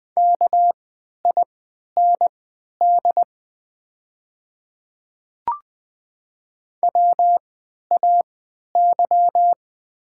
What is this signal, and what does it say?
Morse code, heard as a keyed tone, about 20 wpm, spaced as for 11 wpm, 700 Hz. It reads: KIND WAY
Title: 3x Character Spacing - Top 200 Words - 20wpm